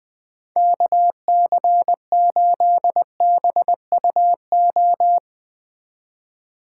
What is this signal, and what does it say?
Morse code, heard as a keyed tone, about 20 wpm, 700 Hz. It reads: KC8BUO